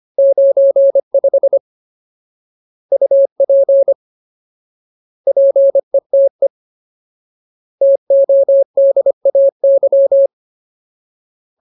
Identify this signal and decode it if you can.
Morse code, heard as a keyed tone, about 25 wpm, 550 Hz. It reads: 95 UP PETE TODAY
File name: Ultimate Word List / 25WPM Ultimate